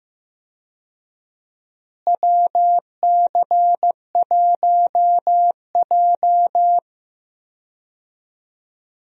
Morse code, heard as a keyed tone, about 15 wpm, 700 Hz. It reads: WC1J
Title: USA Call Signs - 4 Characters - Repeated 3x - 15wpm